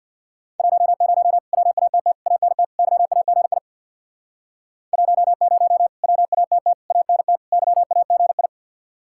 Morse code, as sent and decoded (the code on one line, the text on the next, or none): .---- ----- / .-- .- - - / .- -. - / -.-- .- --. .. / .---- ----- / .-- .- - - / .- -. - / -.-- .- --. ..
10 WATT ANT YAGI 10 WATT ANT YAGI